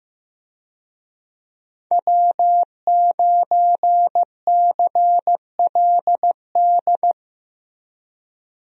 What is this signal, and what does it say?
Morse code, heard as a keyed tone, about 15 wpm, 700 Hz. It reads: W9CLD
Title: USA Call Signs - 5 Characters - Repeated 2x - 15wpm